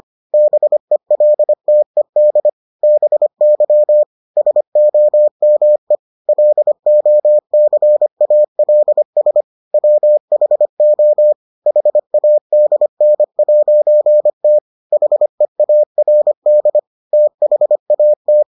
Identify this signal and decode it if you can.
Morse code, heard as a keyed tone, about 25 wpm, 600 Hz. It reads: BELTED BY SOME LOCALS WHO HADN'T HEARD THAT